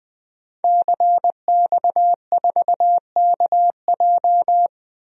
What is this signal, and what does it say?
Morse code, heard as a keyed tone, about 20 wpm, 700 Hz. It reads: CX4KJ
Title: DXCC-100 Call Signs - 5 Characters - 20wpm